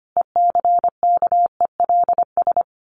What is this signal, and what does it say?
Morse code, heard as a keyed tone, about 25 wpm, 700 Hz. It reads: ECKELS
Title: A Sound Of Thunder 25wpm